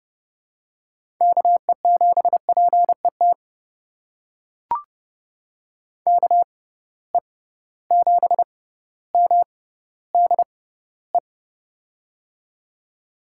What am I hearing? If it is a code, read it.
Morse code, heard as a keyed tone, about 30 wpm, spaced as for 15 wpm, 700 Hz. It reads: KE7PET KE7MDE
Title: USA Call Signs - 6 Characters - 6x Character Spacing - 30wpm